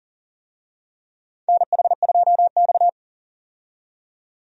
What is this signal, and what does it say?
Morse code, heard as a keyed tone, about 37 wpm, 700 Hz. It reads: NH2X
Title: USA Call Signs - 4 Characters - 40wpm